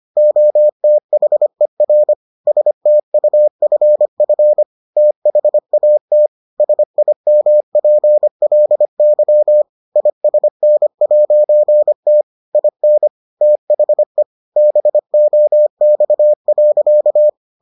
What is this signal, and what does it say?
Morse code, heard as a keyed tone, about 25 wpm, 600 Hz. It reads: OTHER STUFF THAT SIMPLY ISN'T IN THE BOX.